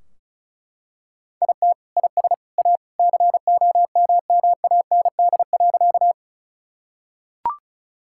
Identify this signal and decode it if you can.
Morse code, heard as a keyed tone, about 36 wpm, 700 Hz. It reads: IT IS A COMMAND.
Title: Sentences from Top 1600 Words 35wpm